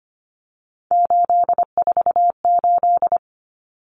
Morse code, heard as a keyed tone, about 25 wpm, 700 Hz. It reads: 848
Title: Numbers - 3-Digit 25wpm